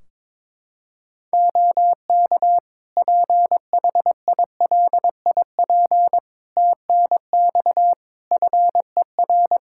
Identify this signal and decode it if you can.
Morse code, heard as a keyed tone, about 22 wpm, 700 Hz. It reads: OK PHILIP TNX FER
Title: Practice QSO - Exchange 4 - Ending - 22wpm